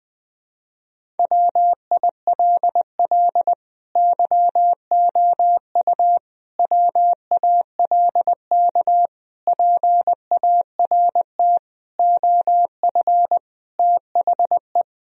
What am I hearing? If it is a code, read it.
Morse code, heard as a keyed tone, about 20 wpm, 700 Hz. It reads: WILL YOU WALK PART OF THE
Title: Sentences from Top 1000 Words - Review Part 1 - 20wpm